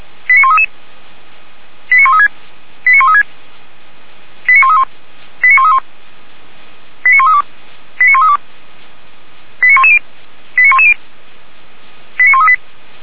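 Five-tone selective call (ZVEI), heard as quick runs of five tones, tones of about 70 ms. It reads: 89133 89136 89136 89131 89131 89132 89132 89110 89110 89138
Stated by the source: Five-tone ZVEI selective call received by SDR, recorded by feuerwehr-orlamuende fire brigade.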